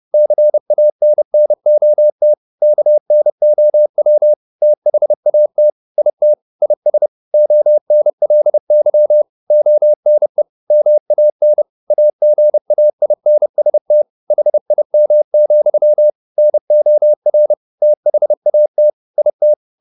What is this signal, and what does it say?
Morse code, heard as a keyed tone, about 30 wpm, 600 Hz. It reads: CANNOT KNOW THAT IT IS ONLY ONE MAN AGAINST HIM, NOR THAT IT
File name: the old man and the sea / the old man and the sea 1 30wpm